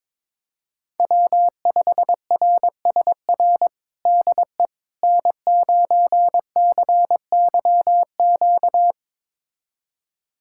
Morse code, heard as a keyed tone, about 22 wpm, 700 Hz. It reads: W5RSR DE N9CYQ
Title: USA Callsign DE Callsign - 5 Characters - 22wpm